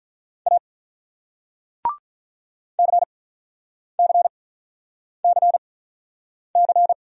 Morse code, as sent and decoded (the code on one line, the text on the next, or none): .- / -.-. -.-. -.-. -.-.
A CCCC